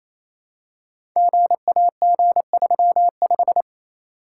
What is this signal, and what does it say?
Morse code, heard as a keyed tone, about 27 wpm, 700 Hz. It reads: GAG35
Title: Code Groups - 5 Letter:Numbers - 28wpm